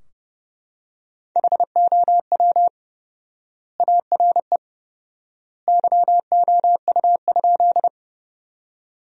Morse code, heard as a keyed tone, about 30 wpm, 700 Hz. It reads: HOW ARE YOU?
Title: 4x Word-Spacing - Sentences from Top 100 Words - 30wpm